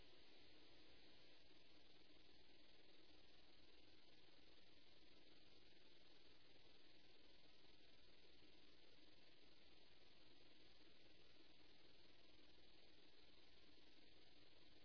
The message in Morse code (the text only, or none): none